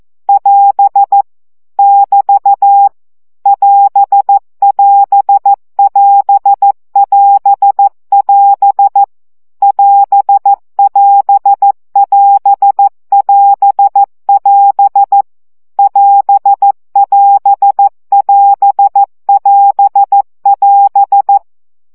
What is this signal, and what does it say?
Morse code, heard as a keyed tone, about 14 wpm, 800 Hz. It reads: <AS> <BT> <AS><AS><AS><AS><AS> <AS><AS><AS><AS><AS> <AS><AS><AS><AS><AS>
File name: CW gyakorlas as 12 baud